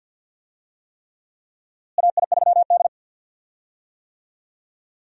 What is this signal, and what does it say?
Morse code, heard as a keyed tone, about 51 wpm, 700 Hz. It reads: AI3D
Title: USA Call Signs - 4 Characters - Repeated 3x - 50wpm